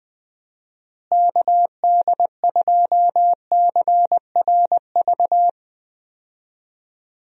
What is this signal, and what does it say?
Morse code, heard as a keyed tone, about 21 wpm, 700 Hz. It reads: KD2CRV
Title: USA Call Signs - 6 Characters - Repeated 2x - 20wpm